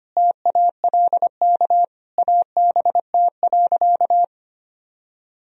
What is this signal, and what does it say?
Morse code, heard as a keyed tone, about 25 wpm, 700 Hz. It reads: TALK ABT.
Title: Sentences from Top 400 Words and Abbreviations 25wpm